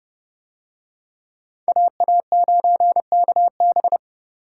Morse code, heard as a keyed tone, about 27 wpm, 700 Hz. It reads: AA9KB